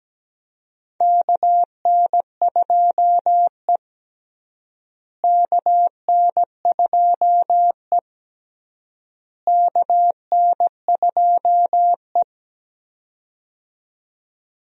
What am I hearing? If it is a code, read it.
Morse code, heard as a keyed tone, about 17 wpm, 700 Hz. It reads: KN2E KN2E KN2E
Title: USA Call Signs - 4 Characters - Repeated 3x - 17wpm